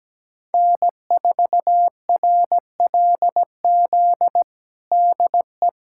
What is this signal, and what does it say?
Morse code, heard as a keyed tone, about 17 wpm, 700 Hz. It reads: N4RLZ DE
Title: USA Callsign DE Callsign 17wpm